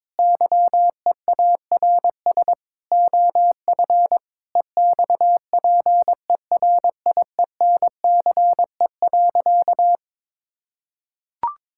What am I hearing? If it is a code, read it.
Morse code, heard as a keyed tone, about 22 wpm, 700 Hz. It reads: YEARS OF EXPERIENCE.
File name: Difficult Sentences with Hints - 22wpm